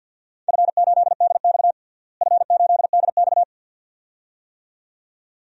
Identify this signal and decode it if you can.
Morse code, heard as a keyed tone, about 51 wpm, 700 Hz. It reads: F8DX F8DX
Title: DXCC-100 Call Signs - 4 Characters - Repeated 2x - 50wpm